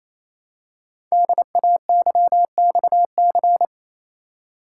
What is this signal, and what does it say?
Morse code, heard as a keyed tone, about 27 wpm, 700 Hz. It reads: DAYXC
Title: Code Groups - 5 Letters - 28wpm